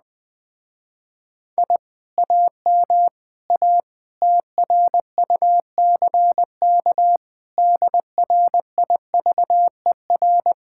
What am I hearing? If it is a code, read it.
Morse code, heard as a keyed tone, about 20 wpm, 700 Hz. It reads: I AM A TRUCK DRIVER
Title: Speed-Racing Common QSO Phrases Encore - 17wpm